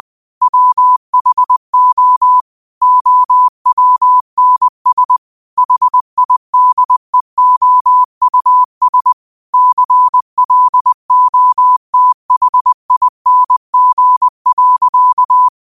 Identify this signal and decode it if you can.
Morse code, heard as a keyed tone, about 20 wpm, 1000 Hz. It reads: WHO OWNS HIDEOUS CLOTHING.